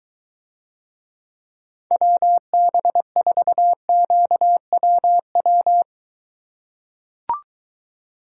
Morse code, heard as a keyed tone, about 23 wpm, 700 Hz. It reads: WB4QWW